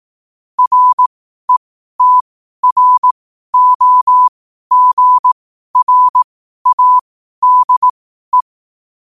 Morse code, heard as a keyed tone, about 18 wpm, spaced as for 12 wpm, 1000 Hz. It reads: RETROGRADE
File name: qotdfiles / 12wpm